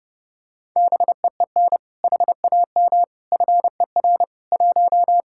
Morse code, heard as a keyed tone, about 30 wpm, 700 Hz. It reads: BEEN HAM FER 1